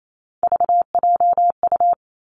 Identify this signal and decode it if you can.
Morse code, heard as a keyed tone, about 28 wpm, 700 Hz. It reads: VJU
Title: Three Challenging Letter-Number Combinations 28wpm